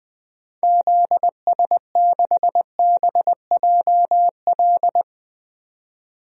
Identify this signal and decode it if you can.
Morse code, heard as a keyed tone, about 20 wpm, 700 Hz. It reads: ZS6BJL
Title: DXCC-100 Call Signs - 6 Characters - 20wpm